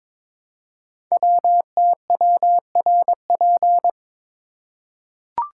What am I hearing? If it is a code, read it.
Morse code, heard as a keyed tone, about 22 wpm, 700 Hz. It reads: WTWRP
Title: Code Groups - 5 Letters - 22wpm